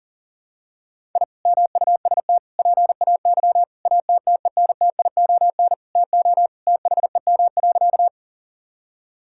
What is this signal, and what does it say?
Morse code, heard as a keyed tone, about 40 wpm, 700 Hz. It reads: I MUST PAY ATTENTION TO THEM.
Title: Sentences from Top 1000 Words - Review Part 6 - 40wpm